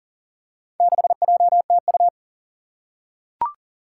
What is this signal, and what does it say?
Morse code, heard as a keyed tone, about 40 wpm, 700 Hz. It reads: 6JTU
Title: Four Challenging Letter-Number Combinations 40wpm